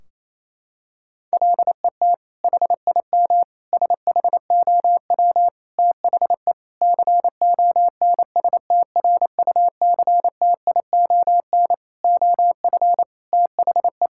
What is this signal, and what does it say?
Morse code, heard as a keyed tone, about 27 wpm, 700 Hz. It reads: LET HIM SHOW THE CONSTRUCTION OF THE
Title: Sentences from Top 1600 Words 28wpm